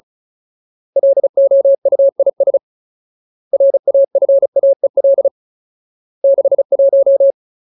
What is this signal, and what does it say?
Morse code, heard as a keyed tone, about 34 wpm, 550 Hz. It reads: LOUIS RAFAEL 61
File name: Ultimate Word List / 35WPM Ultimate